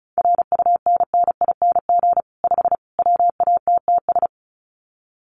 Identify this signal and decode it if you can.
Morse code, heard as a keyed tone, about 34 wpm, 700 Hz. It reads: RUNNING 5 WATTS
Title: Common QSO Phrases 35wpm